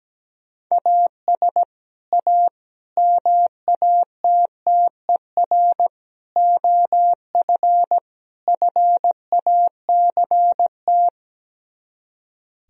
Morse code, heard as a keyed tone, about 17 wpm, 700 Hz. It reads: AS A MATTER OF FACT
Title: Speed-Racing - Sets of 5 Words 17wpm